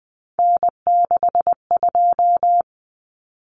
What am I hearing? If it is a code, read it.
Morse code, heard as a keyed tone, about 19 wpm, 700 Hz. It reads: N62
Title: Common Cut Numbers - 3-Digit 20wpm